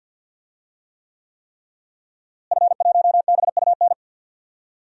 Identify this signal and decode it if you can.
Morse code, heard as a keyed tone, about 53 wpm, 700 Hz. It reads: F1BUN
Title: DXCC-100 Call Signs - 5 Characters - Repeated 3x - 50wpm